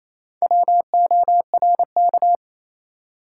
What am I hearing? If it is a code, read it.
Morse code, heard as a keyed tone, about 28 wpm, 700 Hz. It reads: WORK
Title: Four Letter Words - 28wpm